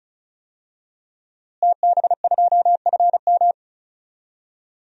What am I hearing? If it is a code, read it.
Morse code, heard as a keyed tone, about 34 wpm, 700 Hz. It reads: TB2FM
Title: DXCC-100 Call Signs - 5 Characters - Repeated 2x - 35wpm